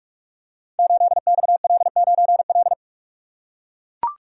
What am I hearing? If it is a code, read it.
Morse code, heard as a keyed tone, about 44 wpm, 700 Hz. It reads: 8XL9L
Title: Code Groups - 5 Letter:Numbers - 45wpm